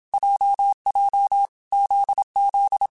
Morse code, heard as a keyed tone, about 27 wpm, 800 Hz. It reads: JJ ZZ
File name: Groups@25 / Groups@25 mp3s / JZ17@25